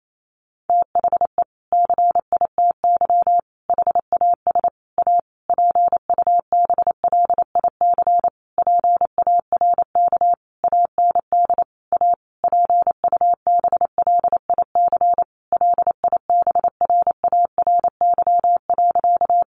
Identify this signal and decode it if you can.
Morse code, heard as a keyed tone, about 28 wpm, 700 Hz. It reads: THE CITY HAS A PUBLIC PARK AND A PUBLIC LIBRARY.
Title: Sentences from Top 1300 Words 28wpm